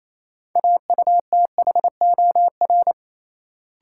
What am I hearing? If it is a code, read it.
Morse code, heard as a keyed tone, about 28 wpm, 700 Hz. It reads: AUTHOR
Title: Sentences from Top 1200 Words - New Word Drill - 28wpm